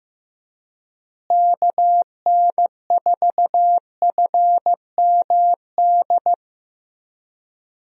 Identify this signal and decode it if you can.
Morse code, heard as a keyed tone, about 15 wpm, 700 Hz. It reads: KN4FMD
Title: USA Call Signs - 6 Characters - Repeated 3x - 15wpm